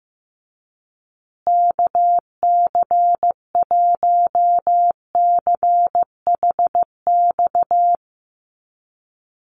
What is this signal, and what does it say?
Morse code, heard as a keyed tone, about 15 wpm, 700 Hz. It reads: KC1CHX
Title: USA Call Signs Repeated 3x 15wpm